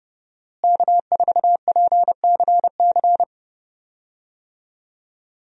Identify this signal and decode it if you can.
Morse code, heard as a keyed tone, about 31 wpm, 700 Hz. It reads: K4PCC
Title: USA Call Signs - 5 Characters - Repeated 3x - 30wpm